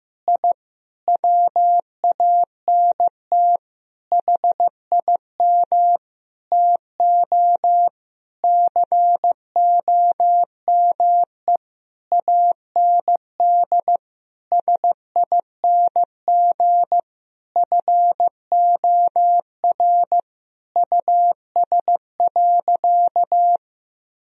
Morse code, heard as a keyed tone, about 15 wpm, 700 Hz. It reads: I WANT HIM TO COME AND SING FOR US.
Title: Sentences from Top 1000 Words - Review Part 2 - 15wpm